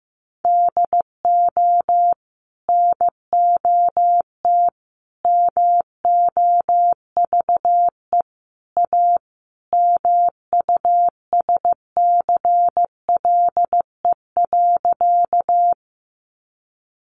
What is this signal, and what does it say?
Morse code, heard as a keyed tone, about 14 wpm, 700 Hz. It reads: DO NOT MOVE A MUSCLE.
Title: Sentences from Top 1300 Words 15wpm